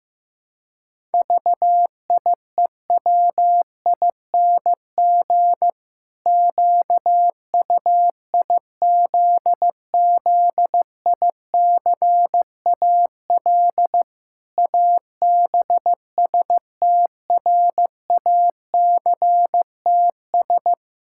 Morse code, heard as a keyed tone, about 15 wpm, 700 Hz. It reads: VIEWING QUIZZICAL ABSTRACTS
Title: Pangram Challenge v2 - 15wpm